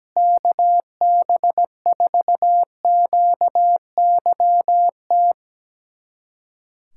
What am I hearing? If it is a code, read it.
Morse code, heard as a keyed tone, about 17 wpm, 700 Hz. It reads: KB4QYT
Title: USA Callsign DE Callsign - 6 Characters - 17wpm